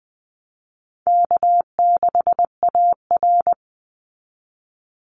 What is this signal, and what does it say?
Morse code, heard as a keyed tone, about 20 wpm, 700 Hz. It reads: K6AR
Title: Call Signs Repeated 2x - 20wpm